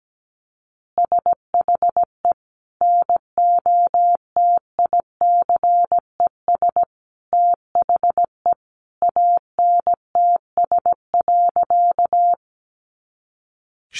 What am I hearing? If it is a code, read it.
Morse code, heard as a keyed tone, about 17 wpm, 700 Hz. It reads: SHE NOTICES THE ANTS.
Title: Easy Phrases 17wpm